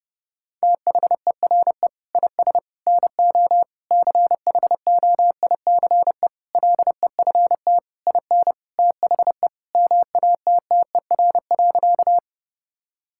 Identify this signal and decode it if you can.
Morse code, heard as a keyed tone, about 30 wpm, 700 Hz. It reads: THERE IS NO CHOICE LEFT IN THE MATTER.
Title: Sentences from Top 1000 Words - Review Part 6 - 30wpm